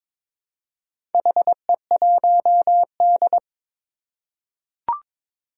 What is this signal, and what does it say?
Morse code, heard as a keyed tone, about 22 wpm, 700 Hz. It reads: HE1D
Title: Four Challenging Letter-Number Combinations 22wpm